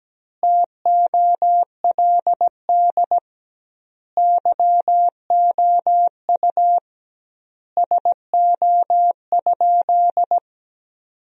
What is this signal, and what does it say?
Morse code, heard as a keyed tone, about 17 wpm, 700 Hz. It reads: TOLD YOU SO?
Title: 2x Word-Spacing - Sentences from Top 400 Words - 17wpm